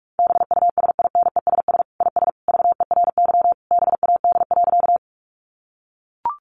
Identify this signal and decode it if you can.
Morse code, heard as a keyed tone, about 44 wpm, 700 Hz. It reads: BUSINESS IS VERY BAD.
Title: Sentences from Top 700 Words 45wpm